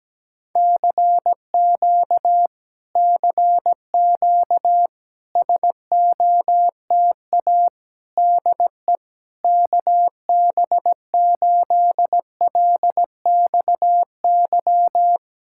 Morse code, heard as a keyed tone, about 17 wpm, 700 Hz. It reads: CQ CQ SOTA DE KB8LXY
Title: Speed-Racing Common QSO Phrases - 17wpm